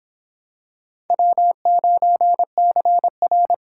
Morse code, heard as a keyed tone, about 26 wpm, 700 Hz. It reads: W9CR